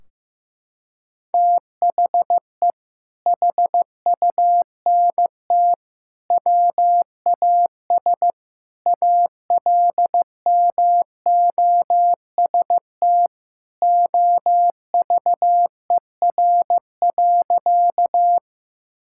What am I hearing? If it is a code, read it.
Morse code, heard as a keyed tone, about 15 wpm, 700 Hz. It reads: THE HUNT WAS ALMOST OVER.
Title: Sentences from Top 1600 Words 15wpm